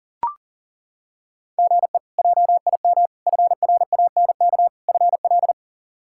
Morse code, heard as a keyed tone, about 40 wpm, 700 Hz. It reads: GE JIM FRANK FL
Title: K1USN SST QSO Practice - 40wpm